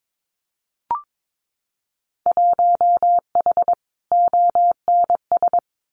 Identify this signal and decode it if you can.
Morse code, heard as a keyed tone, about 22 wpm, 700 Hz. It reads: 1H ONS
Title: Winter Field Day Exchanges 22wpm